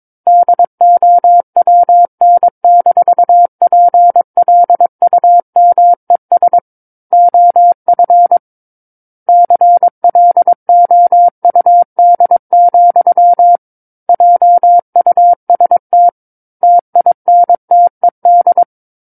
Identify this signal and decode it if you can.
Morse code, heard as a keyed tone, about 22 wpm, 700 Hz. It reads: DOWN-PLUMES OF CLOUD, JUST TINTED
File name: Book One / 13-Chapter 13 - 22 WPM